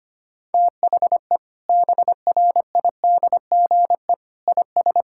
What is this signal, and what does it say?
Morse code, heard as a keyed tone, about 25 wpm, 700 Hz. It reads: THE BRIDGE IS
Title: Sentences from Top 1600 Words 25wpm